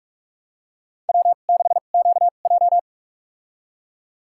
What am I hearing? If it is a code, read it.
Morse code, heard as a keyed tone, about 46 wpm, spaced as for 33 wpm, 700 Hz. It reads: W6QJ
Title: USA Call Signs - 4 Characters - 2x Character Spacing - 45wpm